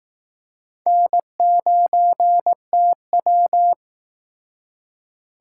Morse code, heard as a keyed tone, about 18 wpm, 700 Hz. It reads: N9TW
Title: Speed Racing - USA 1x2 Call Signs 15wpm